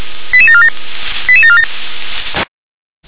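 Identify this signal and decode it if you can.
Five-tone selective call (ZVEI), heard as quick runs of five tones, tones of about 80 ms. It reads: 88747 88747
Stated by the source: ZVEI five-tone selective call received on SDR.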